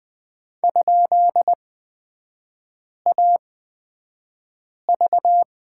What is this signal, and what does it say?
Morse code, heard as a keyed tone, about 20 wpm, spaced as for 5 wpm, 700 Hz. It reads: ?AV
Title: 243 Lesson 39e - Letter z - Three Characters 20wpm